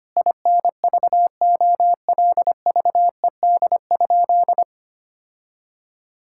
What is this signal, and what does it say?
Morse code, heard as a keyed tone, about 25 wpm, 700 Hz. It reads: INVOLVED?
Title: Sentences from Top 1000 Words - Review Part 5 - 25wpm